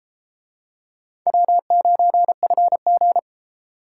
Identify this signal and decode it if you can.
Morse code, heard as a keyed tone, about 30 wpm, 700 Hz. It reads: W9FG